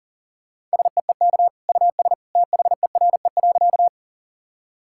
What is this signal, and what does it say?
Morse code, heard as a keyed tone, about 40 wpm, 700 Hz. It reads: SEEK US THERE.
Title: Sentences from Top 1600 Words 40wpm